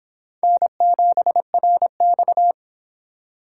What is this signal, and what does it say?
Morse code, heard as a keyed tone, about 26 wpm, 700 Hz. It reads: N7RX